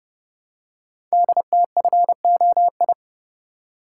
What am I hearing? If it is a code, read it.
Morse code, heard as a keyed tone, about 30 wpm, 700 Hz. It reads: DTFOI
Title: Code Groups - 5 Letters - 30wpm